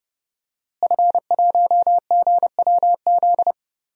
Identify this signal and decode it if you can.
Morse code, heard as a keyed tone, about 31 wpm, 700 Hz. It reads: F1GWZ
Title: DXCC-100 Call Signs - 5 Characters - Repeated 2x - 30wpm